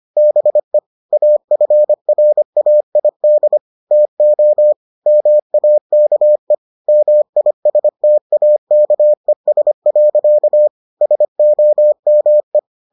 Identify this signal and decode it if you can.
Morse code, heard as a keyed tone, about 25 wpm, 600 Hz. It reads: BE AFRAID TO MAKE MISTAKES. SOME